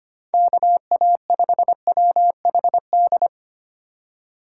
Morse code, heard as a keyed tone, about 25 wpm, 700 Hz. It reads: KA5WHD
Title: USA Call Signs - 6 Characters - 6x Character Spacing - 25wpm